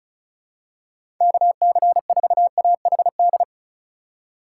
Morse code, heard as a keyed tone, about 36 wpm, 700 Hz. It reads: KC4AHD
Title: USA Call Signs - 6 Characters - 3x Character Spacing - 35wpm